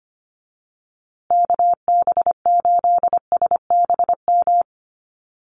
Morse code, heard as a keyed tone, about 25 wpm, 700 Hz. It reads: KB8SBM
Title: USA Call Signs Repeated 2x 25wpm